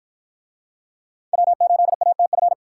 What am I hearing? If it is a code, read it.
Morse code, heard as a keyed tone, about 53 wpm, 700 Hz. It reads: W8ATF